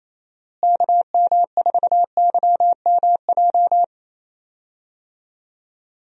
Morse code, heard as a keyed tone, about 28 wpm, 700 Hz. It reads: KM4YMJ
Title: USA Call Signs - 6 Characters - Repeated 2x - 28wpm